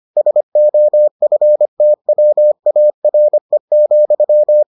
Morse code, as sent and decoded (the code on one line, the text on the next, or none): ... --- ..-. - .-- .- .-. --..--
SOFTWARE,